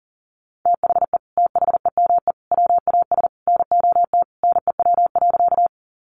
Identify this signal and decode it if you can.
Morse code, heard as a keyed tone, about 40 wpm, 700 Hz. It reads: THE THEME WAS NOT NEW.
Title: Sentences from Top 1100 Words 40wpm